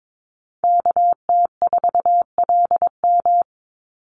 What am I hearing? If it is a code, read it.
Morse code, heard as a keyed tone, about 22 wpm, 700 Hz. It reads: KT4LM
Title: USA Call Signs 22wpm